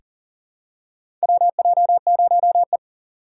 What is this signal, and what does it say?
Morse code, heard as a keyed tone, about 40 wpm, 700 Hz. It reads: WJ0E
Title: USA Call Signs - 4 Characters - 2x Character Spacing - 40wpm